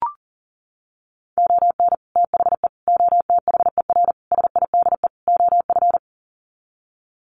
Morse code, heard as a keyed tone, about 37 wpm, 700 Hz. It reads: ON THE OTHER SIDE OF
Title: Sets of 5 Words 40wpm